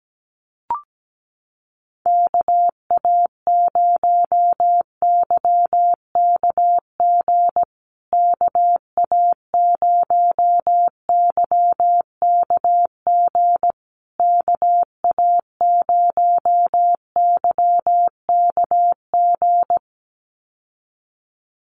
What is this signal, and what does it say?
Morse code, heard as a keyed tone, about 17 wpm, 700 Hz. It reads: KA0YKG KA0YKG KA0YKG
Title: USA Call Signs Repeated 3x 17wpm